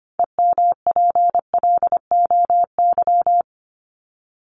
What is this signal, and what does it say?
Morse code, heard as a keyed tone, about 25 wpm, 700 Hz. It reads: EMPLOY
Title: Top 1751-2000 Words 25wpm